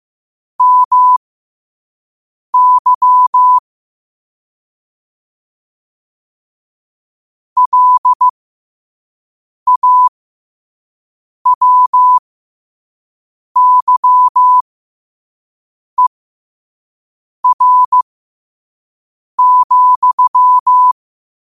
Morse code, heard as a keyed tone, about 15 wpm, spaced as for 5 wpm, 1000 Hz. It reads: MY LAWYER,
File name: qotdfiles / 5wpm